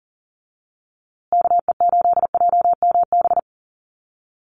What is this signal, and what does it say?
Morse code, heard as a keyed tone, about 40 wpm, 700 Hz. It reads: KE8JMB
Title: USA Call Signs Repeated 3x 40wpm